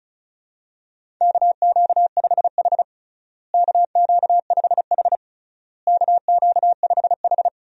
Morse code, heard as a keyed tone, about 36 wpm, 700 Hz. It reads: KQ5H KQ5H KQ5H
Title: USA Call Signs - 4 Characters - Repeated 3x - 35wpm